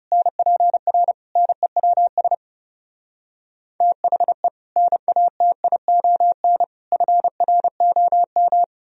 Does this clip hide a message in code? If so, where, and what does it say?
Yes: Morse code from 0.1 to 8.6 s, heard as a keyed tone, about 39 wpm, 700 Hz. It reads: NPR NEWS THE NATION FROM